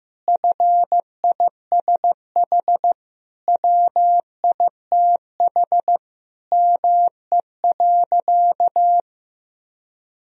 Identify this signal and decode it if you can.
Morse code, heard as a keyed tone, about 15 wpm, 700 Hz. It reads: FISH WITH ME.
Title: Sentences from Top 1000 Words - Review Part 1 - 15wpm